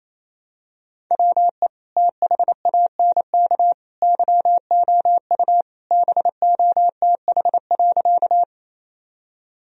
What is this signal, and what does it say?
Morse code, heard as a keyed tone, about 27 wpm, 700 Hz. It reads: WE THANK YOU BOTH.
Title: Sentences from Top 1000 Words - Review Part 1 - 28wpm